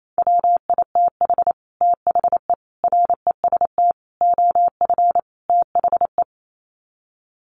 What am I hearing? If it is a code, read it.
Morse code, heard as a keyed tone, about 28 wpm, 700 Hz. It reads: WITH THE REST OF THE
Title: Sets of 5 Words 28wpm